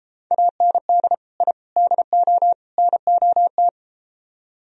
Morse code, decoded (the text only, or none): AND I DO NOT